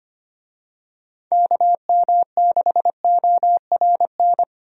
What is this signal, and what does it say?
Morse code, heard as a keyed tone, about 25 wpm, 700 Hz. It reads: KM6ORN
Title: USA Call Signs - 6 Characters - Repeated 2x - 25wpm